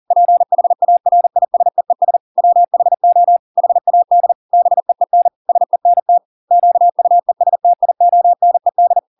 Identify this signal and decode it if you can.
Morse code, heard as a keyed tone, about 40 wpm, 700 Hz. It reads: PHARISEES WHO HAD BEEN SENT QUESTIONED